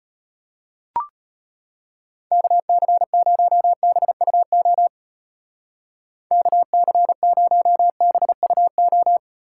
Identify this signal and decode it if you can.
Morse code, heard as a keyed tone, about 30 wpm, 700 Hz. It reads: KC0BUO KC0BUO
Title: Speed Racing - USA 2x3 Call Signs 25wpm